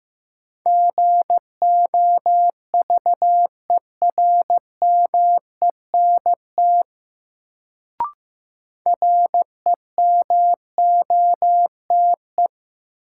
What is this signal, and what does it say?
Morse code, heard as a keyed tone, about 15 wpm, 700 Hz. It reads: GOVERMENT REMOTE
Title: Sentences from Top 1300 Words - New Word Drill - 15wpm